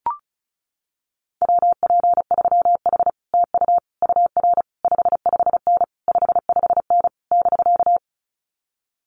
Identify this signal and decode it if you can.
Morse code, heard as a keyed tone, about 36 wpm, 700 Hz. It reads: WP3H TU UR 55N 55N <BK>
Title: Common QSO Phrases Encore 35wpm